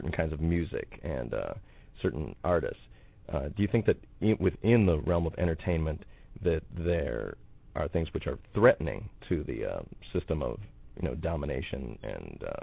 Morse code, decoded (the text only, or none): none